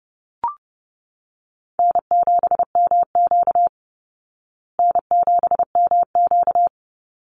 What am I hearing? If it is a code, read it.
Morse code, heard as a keyed tone, about 30 wpm, 700 Hz. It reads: N7MQ N7MQ
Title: Easy Call Signs Repeated 3x - 30wpm